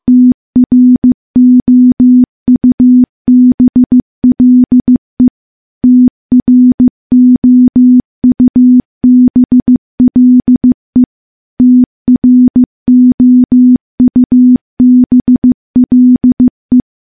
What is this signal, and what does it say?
Morse code, heard as a keyed tone, about 15 wpm, 250 Hz. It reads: TROUBLE TROUBLE TROUBLE
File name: morse-training-15WPM